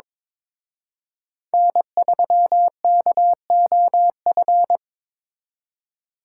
Morse code, heard as a keyed tone, about 26 wpm, 700 Hz. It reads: N3KOF
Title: Speed Racing - USA 1x3 Call Signs 22wpm